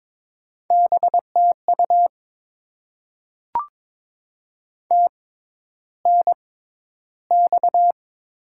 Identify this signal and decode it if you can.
Morse code, heard as a keyed tone, about 22 wpm, spaced as for 11 wpm, 700 Hz. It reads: BTU TNX
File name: Top 20 QSO Elements - 6x Character Spacing - 22wpm